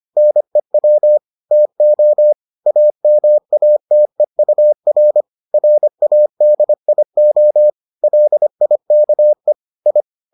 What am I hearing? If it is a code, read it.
Morse code, heard as a keyed tone, about 25 wpm, 600 Hz. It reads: NEW TO AMATEUR RADIO LIKE I